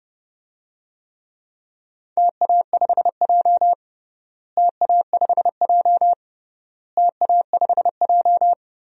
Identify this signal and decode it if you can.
Morse code, heard as a keyed tone, about 32 wpm, 700 Hz. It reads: TA5J TA5J TA5J
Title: DXCC-100 Call Signs - 4 Characters - Repeated 3x - 30wpm